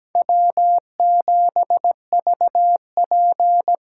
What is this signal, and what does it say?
Morse code, heard as a keyed tone, about 17 wpm, 700 Hz. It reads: W7VP
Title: K1USN SST QSO Practice - 17wpm